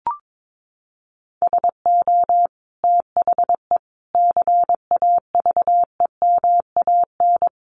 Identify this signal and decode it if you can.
Morse code, heard as a keyed tone, about 22 wpm, 700 Hz. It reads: SO THE CAVEMAN
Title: A Sound Of Thunder 22wpm